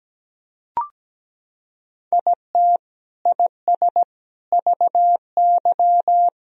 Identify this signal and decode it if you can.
Morse code, heard as a keyed tone, about 17 wpm, 700 Hz. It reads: IT IS VY
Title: Sentences from Top 500 Words and Abbreviations 17wpm